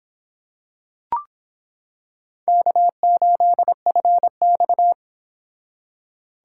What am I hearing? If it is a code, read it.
Morse code, heard as a keyed tone, about 26 wpm, 700 Hz. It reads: K8FX